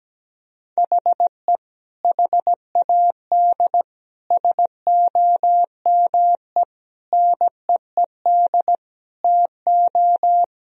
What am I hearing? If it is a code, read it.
Morse code, heard as a keyed tone, about 17 wpm, 700 Hz. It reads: HE HAD SOME NEED TO
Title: Sentences from Top 1000 Words - Review Part 3 - 17wpm